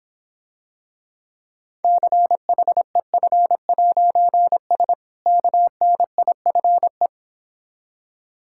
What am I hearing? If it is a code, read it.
Morse code, heard as a keyed tone, about 26 wpm, 700 Hz. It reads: CHEF'S KNIFE